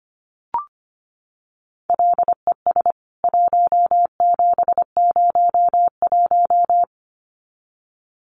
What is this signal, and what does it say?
Morse code, heard as a keyed tone, about 25 wpm, 700 Hz. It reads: LES 1701
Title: CWT Exchange 25wpm